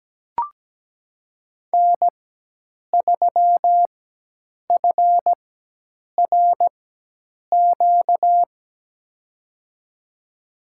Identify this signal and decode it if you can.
Morse code, heard as a keyed tone, about 17 wpm, spaced as for 8 wpm, 700 Hz. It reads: N3FRQ
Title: USA Call Signs - 4x Character Spacing - 17wpm